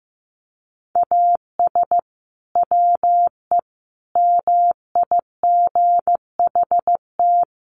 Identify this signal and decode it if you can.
Morse code, heard as a keyed tone, about 15 wpm, 700 Hz. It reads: AS WE MIGHT
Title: Sentences from Top 1100 Words 15wpm